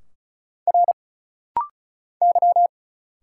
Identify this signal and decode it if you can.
Morse code, heard as a keyed tone, about 34 wpm, 700 Hz. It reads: R Y
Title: Single Letters - Rapid-Fire 35wpm